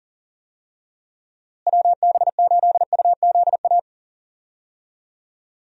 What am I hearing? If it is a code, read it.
Morse code, heard as a keyed tone, about 39 wpm, 700 Hz. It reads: WB8UZA
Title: USA Call Signs - 6 Characters - Repeated 2x - 40wpm